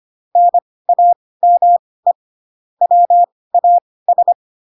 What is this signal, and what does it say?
Morse code, heard as a keyed tone, about 25 wpm, spaced as for 18 wpm, 700 Hz. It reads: NAME WAS
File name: Pinocchio / Pinocchio ch2 12 25 ik1hge